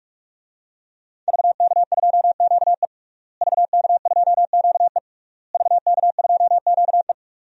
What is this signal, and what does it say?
Morse code, heard as a keyed tone, about 41 wpm, 700 Hz. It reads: VK2QE VK2QE VK2QE
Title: DXCC-100 Call Signs - 5 Characters - Repeated 3x - 45wpm